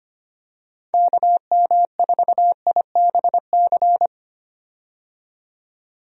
Morse code, heard as a keyed tone, about 25 wpm, 700 Hz. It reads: KM4IBC
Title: USA Call Signs - 6 Characters - 25wpm